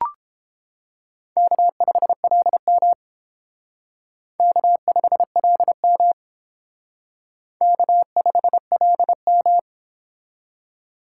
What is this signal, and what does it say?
Morse code, heard as a keyed tone, about 28 wpm, 700 Hz. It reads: K5LM K5LM K5LM